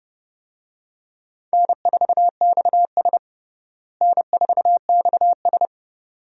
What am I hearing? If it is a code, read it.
Morse code, heard as a keyed tone, about 32 wpm, 700 Hz. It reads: N4XS N4XS
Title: USA Call Signs - 4 Characters - Repeated 2x - 30wpm